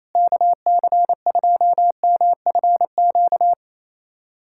Morse code, heard as a keyed tone, about 27 wpm, 700 Hz. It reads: KC2MFQ
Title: USA Callsign DE Callsign 28wpm